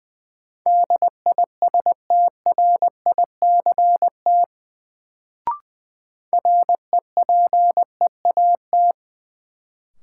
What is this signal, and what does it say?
Morse code, heard as a keyed tone, about 20 wpm, 700 Hz. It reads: DISTRICT REPEAT
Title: Sentences from Top 1600 Words - New Word Drill - 20wpm